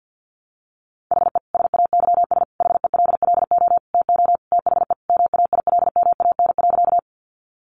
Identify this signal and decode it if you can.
Morse code, heard as a keyed tone, about 50 wpm, 700 Hz. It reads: HE SAYS HELLO TO THE MAILMAN.
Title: Additional Easy Phrases 50wpm